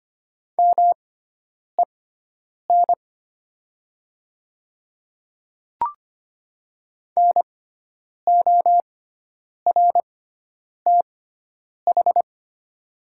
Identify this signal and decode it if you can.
Morse code, heard as a keyed tone, about 25 wpm, spaced as for 9 wpm, 700 Hz. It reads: MEN NORTH
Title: 6x Character Spacing - Top 500 Words - 25wpm